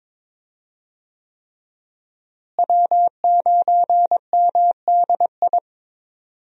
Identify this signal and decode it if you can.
Morse code, heard as a keyed tone, about 22 wpm, 700 Hz. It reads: W9MDI
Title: USA Call Signs - 5 Characters - Repeated 3x - 22wpm